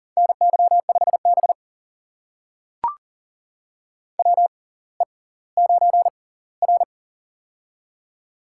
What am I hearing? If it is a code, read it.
Morse code, heard as a keyed tone, about 40 wpm, spaced as for 20 wpm, 700 Hz. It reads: NY5B WE9R
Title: USA Call Signs - 4 Characters - 6x Character Spacing - 40wpm